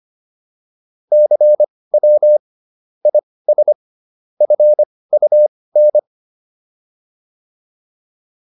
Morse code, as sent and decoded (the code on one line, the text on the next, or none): -.-. .-- / .. ... / ..-. ..- -.
CW IS FUN